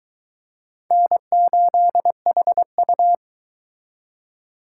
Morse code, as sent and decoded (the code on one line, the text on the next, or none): -. ---.. .... ..-
N8HU